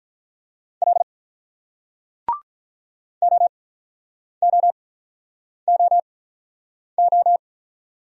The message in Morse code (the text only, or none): L O O O O